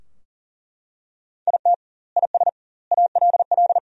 Morse code, heard as a keyed tone, about 40 wpm, 700 Hz. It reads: IT IS ALL
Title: 2x Word-Spacing - Sentences from Top 500 Words - 40wpm